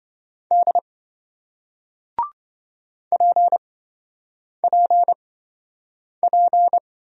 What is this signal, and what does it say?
Morse code, heard as a keyed tone, about 28 wpm, spaced as for 7 wpm, 700 Hz. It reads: D PPP